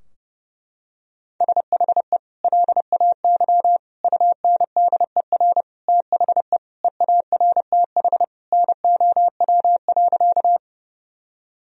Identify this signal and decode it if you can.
Morse code, heard as a keyed tone, about 30 wpm, 700 Hz. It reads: SHE LAY UNDER THE EARTH NOW.
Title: Sentences from Top 1000 Words - Review Part 2 - 30wpm